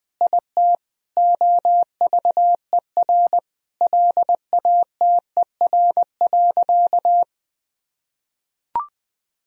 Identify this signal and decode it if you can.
Morse code, heard as a keyed tone, about 20 wpm, 700 Hz. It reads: IT OVER LATER.
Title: Sentences from Top 1000 Words - Review Part 4 - 20wpm